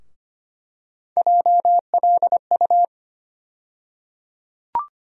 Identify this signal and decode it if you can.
Morse code, heard as a keyed tone, about 25 wpm, 700 Hz. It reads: JLU
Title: USA Call Sign Postfixes - 3 Letters 25wpm